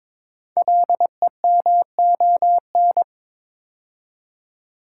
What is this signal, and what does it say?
Morse code, heard as a keyed tone, about 23 wpm, 700 Hz. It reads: LEMON
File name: 5 Letter Words - First 3 Letters Given - 22wpm